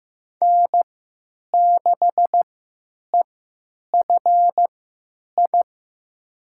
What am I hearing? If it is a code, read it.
Morse code, heard as a keyed tone, about 15 wpm, spaced as for 9 wpm, 700 Hz. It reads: N6EFI
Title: USA Call Signs - 5 Characters - 3x Character Spacing - 15wpm